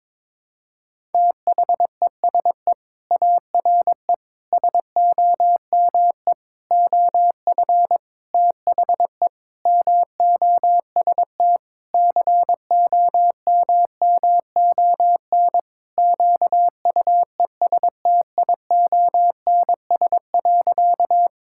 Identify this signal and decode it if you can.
Morse code, heard as a keyed tone, about 22 wpm, 700 Hz. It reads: THESE ARE SOME OF THE MOST COMMON QUESTIONS.
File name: practice / 22wpm / Sentences with 4-Word Phrases - 22wpm